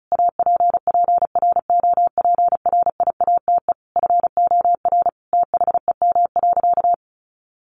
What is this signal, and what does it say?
Morse code, heard as a keyed tone, about 34 wpm, 700 Hz. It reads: APPROPRIATE FOR THEM.
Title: Sentences from Top 900 Words 35wpm